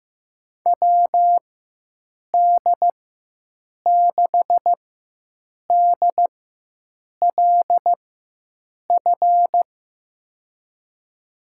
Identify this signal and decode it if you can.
Morse code, heard as a keyed tone, about 15 wpm, spaced as for 7 wpm, 700 Hz. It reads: WD6DLF
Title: USA Call Signs - 4x Character Spacing - 15wpm